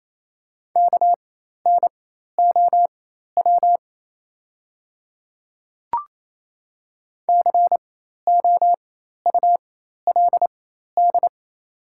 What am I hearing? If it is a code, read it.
Morse code, heard as a keyed tone, about 27 wpm, spaced as for 13 wpm, 700 Hz. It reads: KNOW COULD
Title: 4x Character Spacing - Top 100 Words - 28wpm